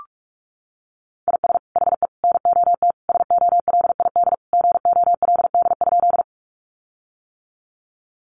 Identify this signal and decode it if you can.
Morse code, heard as a keyed tone, about 44 wpm, 700 Hz. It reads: IS HE NOT SOLID GOLD?
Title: Sentences from Top 1500 Words 45wpm